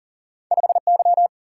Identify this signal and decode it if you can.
Morse code, heard as a keyed tone, about 40 wpm, 700 Hz. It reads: 5 Y
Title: Single-Letter Single-Number 40wpm